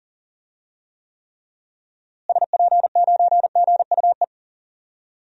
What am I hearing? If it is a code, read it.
Morse code, heard as a keyed tone, about 40 wpm, 700 Hz. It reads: SP9GUE
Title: DXCC-100 Call Signs - 6 Characters - Repeated 3x - 40wpm